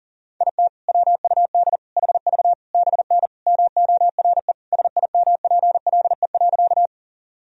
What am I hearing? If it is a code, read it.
Morse code, heard as a keyed tone, about 40 wpm, 700 Hz. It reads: IT WUD HV BN MORE SIMPLE.
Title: Sentences from Top 400 Words and Abbreviations 40wpm